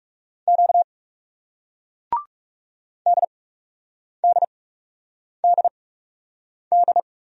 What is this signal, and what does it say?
Morse code, heard as a keyed tone, about 37 wpm, spaced as for 8 wpm, 700 Hz. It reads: Q DDDD
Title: Speed-Racing Single Letters 30wpm